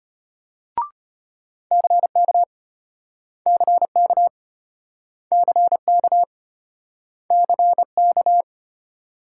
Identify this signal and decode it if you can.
Morse code, heard as a keyed tone, about 33 wpm, 700 Hz. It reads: CK CK CK CK